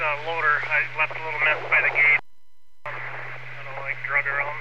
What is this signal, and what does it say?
Emergency Alert System (SAME) tones, heard as harsh buzzing bursts, with speech. It is heard from 0.4 to 3.6 s.